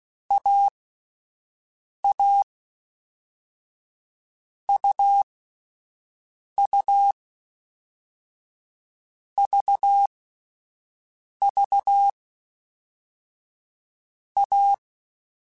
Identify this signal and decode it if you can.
Morse code, heard as a keyed tone, about 16 wpm, 800 Hz. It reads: A A U U V V A
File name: Difficult Combos / Difficult Combos-mp3s / AUV AUV4 @10